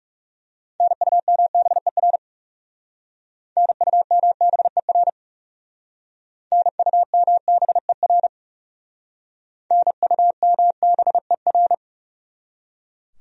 Morse code, heard as a keyed tone, about 38 wpm, 700 Hz. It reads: NUMBER NUMBER NUMBER NUMBER